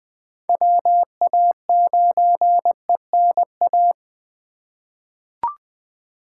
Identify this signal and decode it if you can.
Morse code, heard as a keyed tone, about 20 wpm, 700 Hz. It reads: WA9ENA
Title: K1USN SST QSO Practice - 20wpm